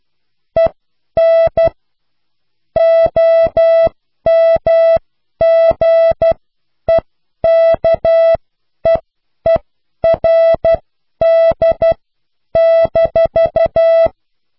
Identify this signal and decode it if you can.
Morse code, heard as a keyed tone, about 12 wpm, spaced as for 10 wpm, 660 Hz. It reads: EN OMGEKEERD-